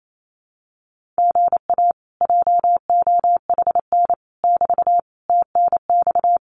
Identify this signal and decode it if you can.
Morse code, heard as a keyed tone, about 28 wpm, 700 Hz. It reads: GA JOHN <BT> TNX
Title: Common QSO Phrases Encore 28wpm